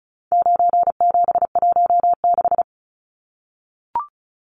Morse code, heard as a keyed tone, about 34 wpm, 700 Hz. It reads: 9716
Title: Numbers - 4-Digit 35wpm